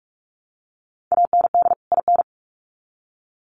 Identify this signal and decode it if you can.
Morse code, heard as a keyed tone, about 44 wpm, 700 Hz. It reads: AND IN
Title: Sets of 2 Words 45wpm